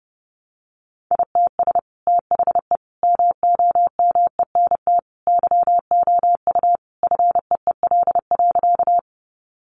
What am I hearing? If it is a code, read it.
Morse code, heard as a keyed tone, about 30 wpm, 700 Hz. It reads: ITS THE MOMENT YOU FEEL.
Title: Sentences from Top 1000 Words 30wpm